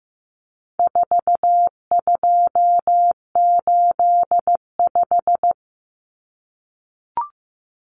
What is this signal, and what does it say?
Morse code, heard as a keyed tone, about 15 wpm, 700 Hz. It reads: 4285
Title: Common Cut Numbers - 4-Digit 15wpm